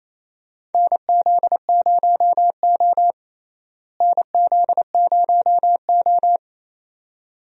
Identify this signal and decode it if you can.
Morse code, heard as a keyed tone, about 28 wpm, 700 Hz. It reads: NZ0O NZ0O
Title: USA Call Signs - 4 Characters - Repeated 2x - 28wpm